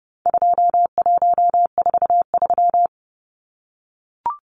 Morse code, heard as a keyed tone, about 30 wpm, 700 Hz. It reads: 2143
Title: Numbers - 4-Digit 30wpm